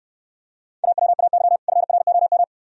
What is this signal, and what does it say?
Morse code, heard as a keyed tone, about 69 wpm, 690 Hz. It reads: AWAY FROM